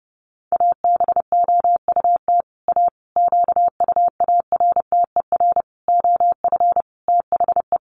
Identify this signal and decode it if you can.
Morse code, heard as a keyed tone, about 30 wpm, 700 Hz. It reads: ABOUT A QUARTER OF THE
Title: Sentences from Top 1300 Words 30wpm